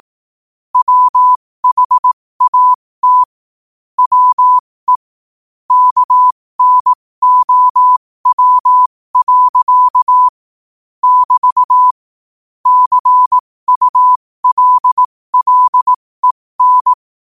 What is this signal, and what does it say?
Morse code, heard as a keyed tone, about 18 wpm, spaced as for 15 wpm, 1000 Hz. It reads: WHAT WE KNOW. <BT> CULLEN
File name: qotdfiles / 15wpm